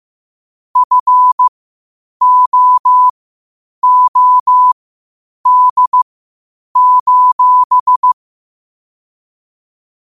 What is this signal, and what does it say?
Morse code, heard as a keyed tone, about 15 wpm, spaced as for 8 wpm, 1000 Hz. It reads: FOOD:
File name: qotdfiles / 8wpm